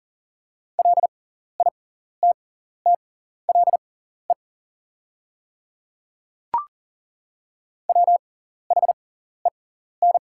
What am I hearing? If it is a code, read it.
Morse code, heard as a keyed tone, about 40 wpm, spaced as for 14 wpm, 700 Hz. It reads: LITTLE WHEN
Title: 6x Character Spacing - Top 200 Words - 40wpm